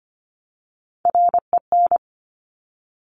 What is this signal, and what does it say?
Morse code, heard as a keyed tone, about 25 wpm, 700 Hz. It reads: REN
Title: Top Three Letter Combinations in Words 25wpm